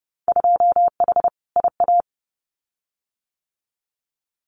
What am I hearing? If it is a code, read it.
Morse code, heard as a keyed tone, about 30 wpm, 700 Hz. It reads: 2H IA
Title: Winter Field Day Exchanges Encore 30wpm